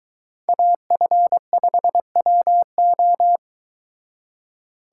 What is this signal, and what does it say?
Morse code, heard as a keyed tone, about 22 wpm, 700 Hz. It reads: AF5WO